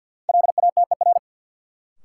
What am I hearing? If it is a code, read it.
Morse code, heard as a keyed tone, about 52 wpm, 700 Hz. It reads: LATER
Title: Sentences from Top 700 Words - New Word Drill - 50wpm